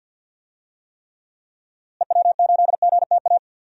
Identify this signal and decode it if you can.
Morse code, heard as a keyed tone, about 45 wpm, 700 Hz. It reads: EW8GTA
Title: DXCC-100 Call Signs - 6 Characters - Repeated 3x - 50wpm